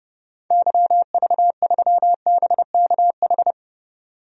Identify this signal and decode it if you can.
Morse code, heard as a keyed tone, about 30 wpm, 700 Hz. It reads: YV3BKH
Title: DXCC-100 Call Signs - 6 Characters - 30wpm